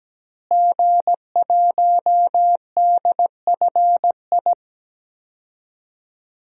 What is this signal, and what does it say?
Morse code, heard as a keyed tone, about 17 wpm, 700 Hz. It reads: G1DFI
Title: DXCC-100 Call Signs - 5 Characters - Repeated 3x - 17wpm